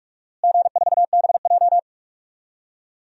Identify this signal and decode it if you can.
Morse code, heard as a keyed tone, about 46 wpm, 700 Hz. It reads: G4BJ
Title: DXCC-100 Call Signs - 4 Characters - Repeated 2x - 45wpm